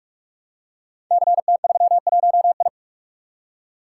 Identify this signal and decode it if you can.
Morse code, heard as a keyed tone, about 44 wpm, 690 Hz. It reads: CT31I
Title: Code Groups - 5 Letter:Numbers - 45wpm